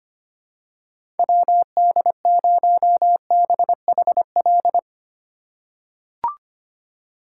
Morse code, heard as a keyed tone, about 25 wpm, 700 Hz. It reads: WD0BHL
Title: USA Call Signs - 6 Characters - 2x Character Spacing - 25wpm